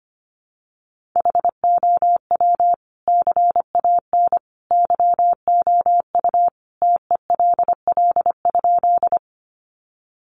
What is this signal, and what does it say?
Morse code, heard as a keyed tone, about 25 wpm, 700 Hz. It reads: HOW CAN YOU TELL?
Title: Sentences from Top 200 Words 25wpm